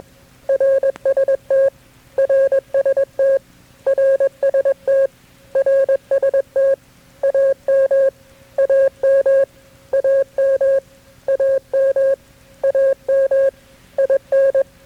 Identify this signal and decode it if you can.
Morse code, heard as a keyed tone, about 21 wpm, 540 Hz. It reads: RST RST RST RST AM AM AM AM AM IN
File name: KB6MT-tape3-side1